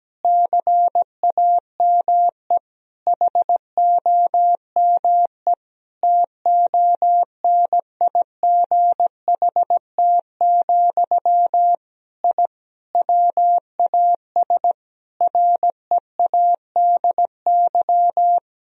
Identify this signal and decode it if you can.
Morse code, heard as a keyed tone, about 17 wpm, 700 Hz. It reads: CAME HOME TONIGHT, I WAS READY
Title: Sentences from Top 1000 Words - Review Part 3 - 17wpm